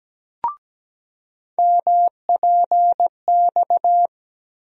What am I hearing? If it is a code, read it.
Morse code, heard as a keyed tone, about 17 wpm, 700 Hz. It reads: MPX
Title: USA Call Sign Postfixes - 3 Letters 17wpm